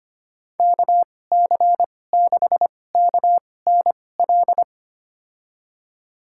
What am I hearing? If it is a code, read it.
Morse code, heard as a keyed tone, about 25 wpm, spaced as for 18 wpm, 700 Hz. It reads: KC6KNL
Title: USA Call Signs - 6 Characters - 2x Character Spacing - 25wpm